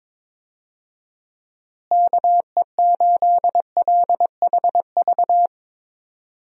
Morse code, heard as a keyed tone, about 22 wpm, 700 Hz. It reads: KE8LHV
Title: USA Call Signs - 6 Characters - Repeated 3x - 22wpm